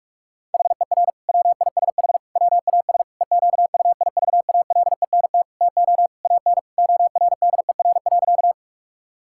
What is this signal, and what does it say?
Morse code, heard as a keyed tone, about 46 wpm, 700 Hz. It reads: HER WISH WAS EQUIVALENT TO AN ORDER.
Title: Sentences from Top 1600 Words 45wpm